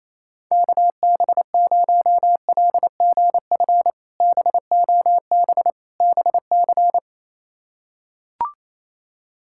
Morse code, heard as a keyed tone, about 28 wpm, 700 Hz. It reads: KB0LGF BOB BC
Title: K1USN SST QSO Practice - 28wpm